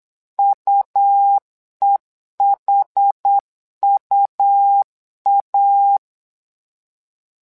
Morse code, heard as a keyed tone, about 8 wpm, 800 Hz. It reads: UEHUA
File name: LEZIONI / 2 UHDN / 2 UH 8wpm 2129